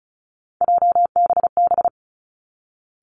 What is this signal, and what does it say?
Morse code, heard as a keyed tone, about 36 wpm, 700 Hz. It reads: JBB
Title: Three Challenging Letter-Number Combinations 35wpm